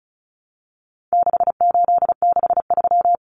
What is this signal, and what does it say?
Morse code, heard as a keyed tone, about 34 wpm, 700 Hz. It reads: 6863
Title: Common Cut Numbers - 4-Digit 35wpm